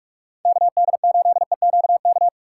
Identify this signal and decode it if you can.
Morse code, heard as a keyed tone, about 46 wpm, 700 Hz. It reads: KD8EQK